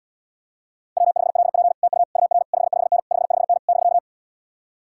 Morse code, heard as a keyed tone, about 77 wpm, 700 Hz. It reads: W5LP TU UR 55N 55N <BK>